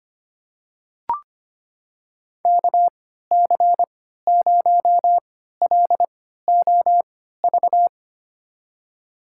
Morse code, heard as a keyed tone, about 25 wpm, spaced as for 14 wpm, 700 Hz. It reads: KC0LOV
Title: USA Call Signs - 3x Character Spacing - 25wpm